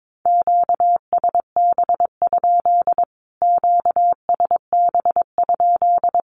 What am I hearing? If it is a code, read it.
Morse code, heard as a keyed tone, about 22 wpm, 700 Hz. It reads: QSB? QSB?
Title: Q-Codes Repeated 3x 22wpm